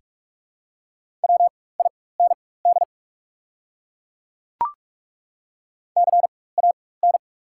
Morse code, heard as a keyed tone, about 43 wpm, spaced as for 21 wpm, 690 Hz. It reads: WIND CAN
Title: 4x Character Spacing - Top 500 Words - 45wpm